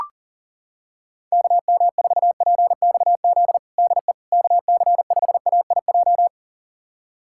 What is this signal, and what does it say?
Morse code, heard as a keyed tone, about 40 wpm, 700 Hz. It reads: KM4PXZ DE KC5AIJ
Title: USA Callsign DE Callsign - 6 Characters - 40wpm